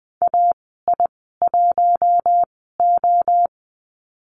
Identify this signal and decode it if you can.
Morse code, heard as a keyed tone, about 20 wpm, spaced as for 14 wpm, 700 Hz. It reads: AI1O
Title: 042 Lesson 8g - Number 1 - Call Signs 20wpm